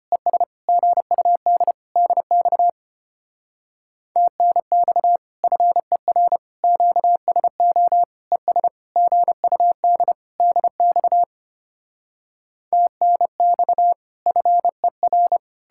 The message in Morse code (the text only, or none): ES GUD DX TNX FER QSO ES GUD DX TNX FER